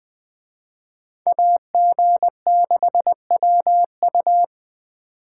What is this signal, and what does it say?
Morse code, heard as a keyed tone, about 23 wpm, 700 Hz. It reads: AG6WU
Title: Speed Racing - USA Call Signs 20wpm